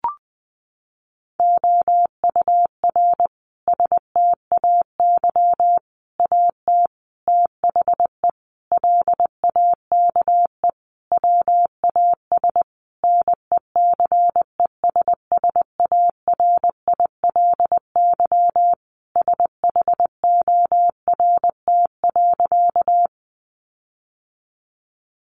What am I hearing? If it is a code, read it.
Morse code, heard as a keyed tone, about 20 wpm, 700 Hz. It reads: OUR STAY AT THE LAKE WAS NECESSARILY SHORT.
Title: Sentences from Top 1400 Words 20wpm